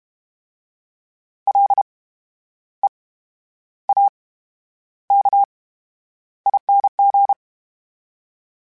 Morse code, heard as a keyed tone, about 32 wpm, 800 Hz. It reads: L E A K ING
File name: Special INGs / Special INGs-mp3s / VerySlowToFaster Ing @30